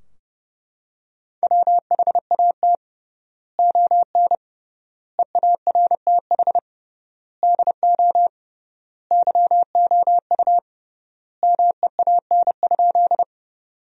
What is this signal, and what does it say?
Morse code, heard as a keyed tone, about 30 wpm, 700 Hz. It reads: WHAT ON EARTH DO YOU MEAN?
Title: 3x Word-Spacing - Sentences from Top 200 Words - 30wpm